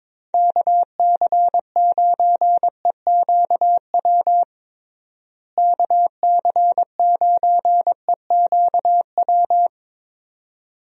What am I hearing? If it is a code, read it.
Morse code, heard as a keyed tone, about 21 wpm, 700 Hz. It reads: KC9EQW KC9EQW
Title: USA Call Signs - 6 Characters - Repeated 2x - 22wpm